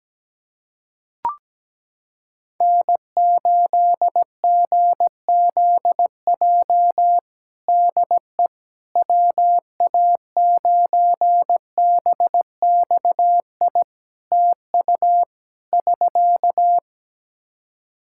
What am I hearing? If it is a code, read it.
Morse code, heard as a keyed tone, about 17 wpm, 700 Hz. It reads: N8GZJ DE WA9BXI TU <SK>
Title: K1USN SST QSO Practice 17wpm